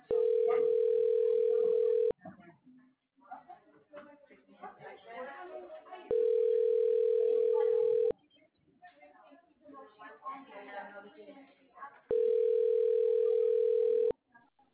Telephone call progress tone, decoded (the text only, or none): ringback tone